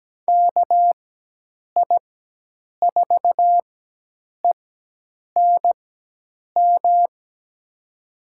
Morse code, heard as a keyed tone, about 17 wpm, spaced as for 8 wpm, 700 Hz. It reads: KI4ENM
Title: USA Call Signs - 6 Characters - 4x Character Spacing - 17wpm